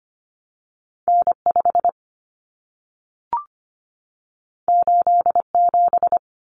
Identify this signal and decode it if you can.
Morse code, heard as a keyed tone, about 25 wpm, 700 Hz. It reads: N 5 8 7
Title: Common Cut Numbers - 2-Digit 25wpm